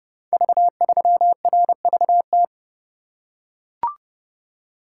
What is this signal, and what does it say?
Morse code, heard as a keyed tone, about 30 wpm, 700 Hz. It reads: V3RVT
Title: Code Groups - 5 Letter:Numbers - 30wpm